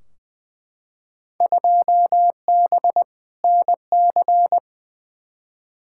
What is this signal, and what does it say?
Morse code, heard as a keyed tone, about 20 wpm, 700 Hz. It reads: 2B NC
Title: Field Day Exchanges Encore - 20wpm